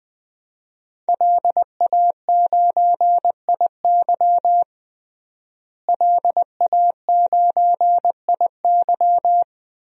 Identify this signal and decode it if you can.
Morse code, heard as a keyed tone, about 20 wpm, 700 Hz. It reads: LA9IY LA9IY
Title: DXCC-100 Call Signs - 5 Characters - Repeated 2x - 20wpm